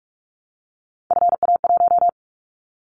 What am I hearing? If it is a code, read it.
Morse code, heard as a keyed tone, about 46 wpm, 700 Hz. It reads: FA1
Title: Callsign Prefixes for Top 25 DXCC - 45wpm